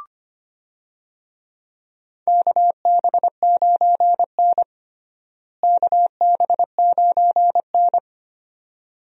Morse code, heard as a keyed tone, about 25 wpm, 700 Hz. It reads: KB9N KB9N
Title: USA Call Signs - 4 Characters - Repeated 2x - 25wpm